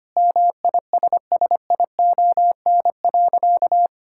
Morse code, heard as a keyed tone, about 25 wpm, 700 Hz. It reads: MISSION.
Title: Sentences from Top 1600 Words 25wpm